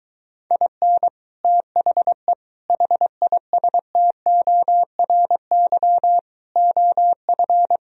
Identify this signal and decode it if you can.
Morse code, heard as a keyed tone, about 23 wpm, 700 Hz. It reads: IN THE HISTORY OF